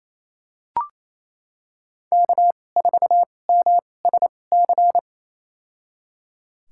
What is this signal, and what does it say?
Morse code, heard as a keyed tone, about 28 wpm, spaced as for 20 wpm, 700 Hz. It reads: K4MSC
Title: USA Call Signs - 5 Characters - 2x Character Spacing - 28wpm